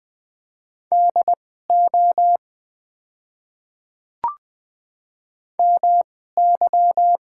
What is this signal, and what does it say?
Morse code, heard as a keyed tone, about 20 wpm, spaced as for 14 wpm, 700 Hz. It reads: DO MY
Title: 2x Character Spacing - Top 100 Words - 20wpm